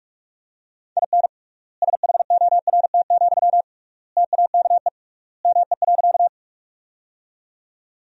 Morse code, heard as a keyed tone, about 48 wpm, 700 Hz. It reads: IN SHORT, TAKE ME.
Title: 3x Word-Spacing - Sentences from Top 400 Words - 45wpm